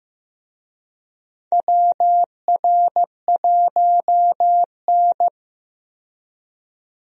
Morse code, heard as a keyed tone, about 15 wpm, 700 Hz. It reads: WR1N
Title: USA Call Signs - 4 Characters - 15wpm